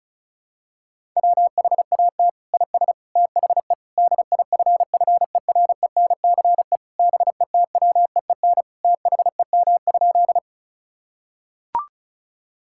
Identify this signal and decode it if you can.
Morse code, heard as a keyed tone, about 36 wpm, 700 Hz. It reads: WHAT IS THE DIFFERENCE BETWEEN THEM?
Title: Sentences from Top 1000 Words - Review Part 5 - 35wpm